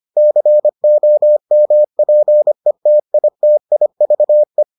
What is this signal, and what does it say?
Morse code, heard as a keyed tone, about 25 wpm, 600 Hz. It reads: COMPETITIVE